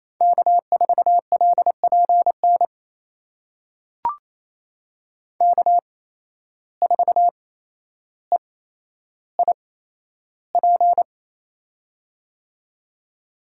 Morse code, heard as a keyed tone, about 27 wpm, spaced as for 12 wpm, 700 Hz. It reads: K4LPN K4EIP
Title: USA Call Signs - 5 Characters - 8x Character Spacing - 28wpm